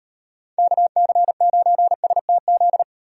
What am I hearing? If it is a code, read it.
Morse code, heard as a keyed tone, about 36 wpm, 700 Hz. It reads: KC9STZ